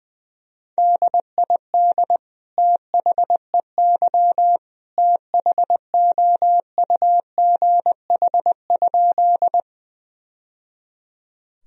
Morse code, heard as a keyed tone, about 21 wpm, 700 Hz. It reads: DID THEY THOUGH?
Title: Sentences from Top 1000 Words - Review Part 2 - 20wpm